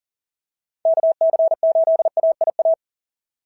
Morse code, heard as a keyed tone, about 40 wpm, 660 Hz. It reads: KC8AIA
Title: SOTA QSO Practice 40wpm